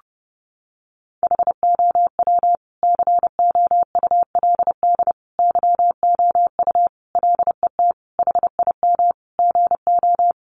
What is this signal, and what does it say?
Morse code, heard as a keyed tone, about 30 wpm, 700 Hz. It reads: HOW COULD YOU LET HIM GO
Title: Sentences from Top 300 Words 30wpm